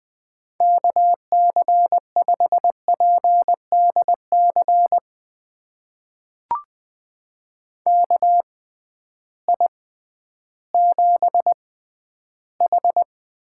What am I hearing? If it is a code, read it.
Morse code, heard as a keyed tone, about 20 wpm, spaced as for 12 wpm, 700 Hz. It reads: KC5PDC KI7H
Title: USA Call Signs - 6 Characters - 6x Character Spacing - 20wpm